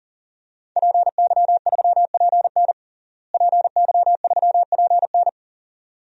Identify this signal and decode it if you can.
Morse code, heard as a keyed tone, about 40 wpm, 700 Hz. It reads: PY3PN PY3PN
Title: DXCC-100 Call Signs - 5 Characters - Repeated 2x - 40wpm